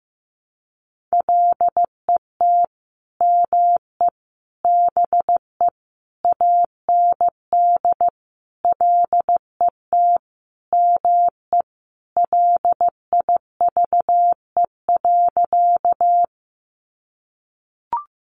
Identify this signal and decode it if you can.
Morse code, heard as a keyed tone, about 15 wpm, 700 Hz. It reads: LET ME BE AND LET ME LIVE.
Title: Sentences from Top 300 Words 15wpm